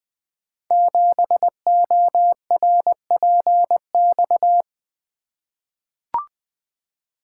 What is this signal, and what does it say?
Morse code, heard as a keyed tone, about 20 wpm, 700 Hz. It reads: 7ORPX
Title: Code Groups - 5 Letter:Numbers - 20wpm